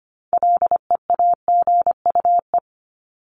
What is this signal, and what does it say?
Morse code, heard as a keyed tone, about 25 wpm, 700 Hz. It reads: LEAGUE
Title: Top 1751-2000 Words 25wpm